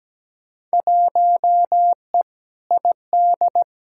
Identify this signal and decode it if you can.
Morse code, heard as a keyed tone, about 17 wpm, 700 Hz. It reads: 1E ID
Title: Field Day Exchanges - 17wpm